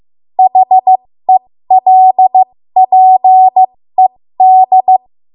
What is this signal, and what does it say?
Morse code, heard as a keyed tone, about 15 wpm, spaced as for 13 wpm, 750 Hz. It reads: HELPED
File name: Morse / 13WPM